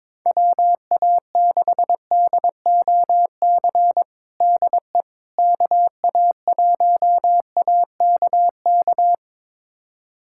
Morse code, heard as a keyed tone, about 22 wpm, 700 Hz. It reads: WA6DOC DE KA1AKK
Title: USA Callsign DE Callsign - 6 Characters - 22wpm